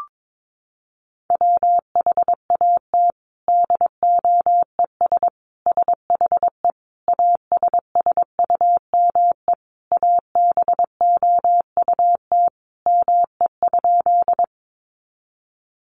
Morse code, heard as a keyed tone, about 22 wpm, 700 Hz. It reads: WHAT DOES SHE ASSUME ABOUT ME?
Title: Sentences from Top 1500 Words 22wpm